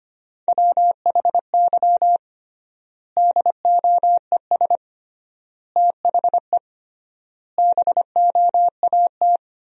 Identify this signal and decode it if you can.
Morse code, heard as a keyed tone, about 25 wpm, 700 Hz. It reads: WHY DOES THE BOAT
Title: 3x Word-Spacing - Sentences from Top 500 Words - 25wpm